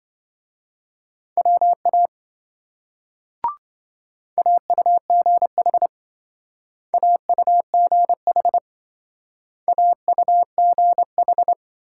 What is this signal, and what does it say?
Morse code, heard as a keyed tone, about 26 wpm, 700 Hz. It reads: WA AUGH AUGH AUGH